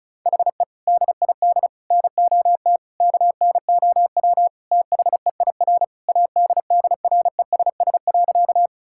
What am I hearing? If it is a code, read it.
Morse code, heard as a keyed tone, about 36 wpm, 700 Hz. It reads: HE DID NOT KNOW THEIR ADDRESS.
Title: Sentences from Top 1000 Words - Review Part 6 - 35wpm